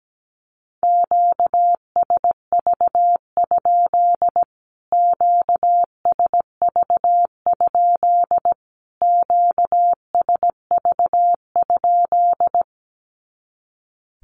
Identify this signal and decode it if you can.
Morse code, heard as a keyed tone, about 17 wpm, 700 Hz. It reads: QSV? QSV? QSV?
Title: Q-Codes Repeated 3x 17wpm